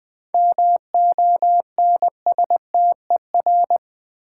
Morse code, heard as a keyed tone, about 20 wpm, 700 Hz. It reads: MONSTER
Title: 7 Letter Words - First 3 Letters Given - 20wpm